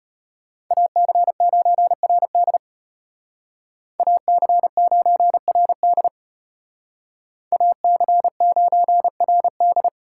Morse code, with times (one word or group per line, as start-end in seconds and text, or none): 0.7-2.6 AC9RD
4.0-6.1 AC9RD
7.5-9.9 AC9RD